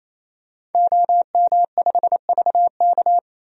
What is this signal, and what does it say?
Morse code, heard as a keyed tone, about 28 wpm, 700 Hz. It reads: OM5VK
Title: DXCC-100 Call Signs - 5 Characters - 28wpm